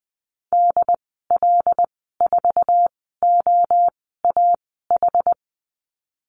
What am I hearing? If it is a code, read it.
Morse code, heard as a keyed tone, about 20 wpm, spaced as for 14 wpm, 700 Hz. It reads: DL4OAH
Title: 076 Lesson 13g - Letter L - Call Signs 20wpm